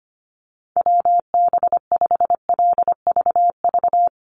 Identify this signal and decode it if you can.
Morse code, heard as a keyed tone, about 25 wpm, 700 Hz. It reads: WB5LVV
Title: USA Call Signs Repeated 2x 25wpm